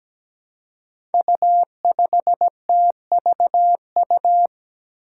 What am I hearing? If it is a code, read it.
Morse code, heard as a keyed tone, about 17 wpm, 700 Hz. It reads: U5TVU
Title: Five Challenging Letter-Number Combinations 17wpm